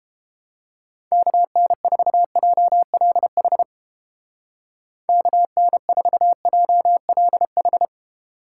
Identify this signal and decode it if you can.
Morse code, heard as a keyed tone, about 31 wpm, 700 Hz. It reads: KN4JLH KN4JLH